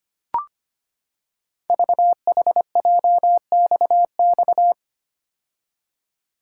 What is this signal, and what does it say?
Morse code, heard as a keyed tone, about 25 wpm, 700 Hz. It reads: VHJXX
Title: Code Groups - 5 Letters - 25wpm